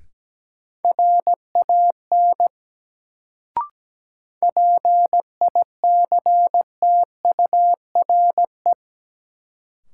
Morse code, heard as a keyed tone, about 17 wpm, 700 Hz. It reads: RAN PICTURE
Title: Sentences from Top 500 Words - New Word Drill - 17wpm